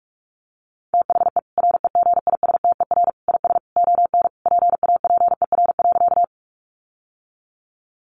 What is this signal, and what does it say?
Morse code, heard as a keyed tone, about 44 wpm, 700 Hz. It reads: THE REGISTER IS ON PAPER.
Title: Sentences from Top 1200 Words 45wpm